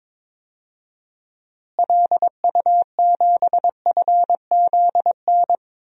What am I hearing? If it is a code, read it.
Morse code, heard as a keyed tone, about 22 wpm, 700 Hz. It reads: LU7FZN
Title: DXCC-100 Call Signs - 6 Characters - Repeated 2x - 22wpm